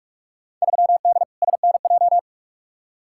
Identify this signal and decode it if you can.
Morse code, heard as a keyed tone, about 48 wpm, 700 Hz. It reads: 3D SNJ
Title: Field Day Exchanges Encore - 45wpm